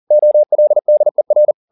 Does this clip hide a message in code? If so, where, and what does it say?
Yes: Morse code from 0.1 to 1.5 s, heard as a keyed tone, about 40 wpm, 600 Hz. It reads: OLDER